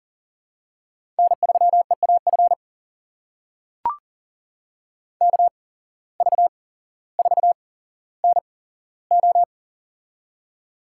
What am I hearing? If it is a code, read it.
Morse code, heard as a keyed tone, about 39 wpm, spaced as for 17 wpm, 700 Hz. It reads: N3EAF KV4NO
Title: USA Call Signs - 5 Characters - 8x Character Spacing - 40wpm